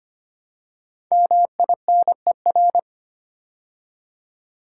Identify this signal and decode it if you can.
Morse code, heard as a keyed tone, about 25 wpm, 700 Hz. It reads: MINER
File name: 5 Letter Words - First 3 Letters Given - 25wpm